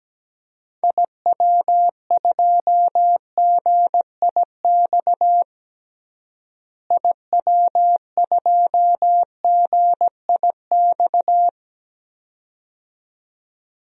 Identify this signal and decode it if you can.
Morse code, heard as a keyed tone, about 17 wpm, 700 Hz. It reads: IW2GIX IW2GIX
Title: DXCC-100 Call Signs - 6 Characters - Repeated 2x - 17wpm